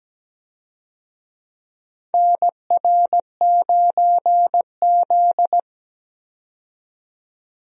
Morse code, heard as a keyed tone, about 17 wpm, 700 Hz. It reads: NR9Z
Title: USA Call Signs - 4 Characters - 17wpm